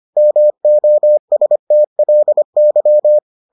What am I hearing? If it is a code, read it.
Morse code, heard as a keyed tone, about 25 wpm, 600 Hz. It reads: MOSTLY